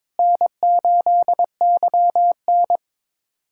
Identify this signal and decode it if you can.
Morse code, heard as a keyed tone, about 22 wpm, 700 Hz. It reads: N8YN
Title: USA Callsign DE Callsign - 4 Characters - 22wpm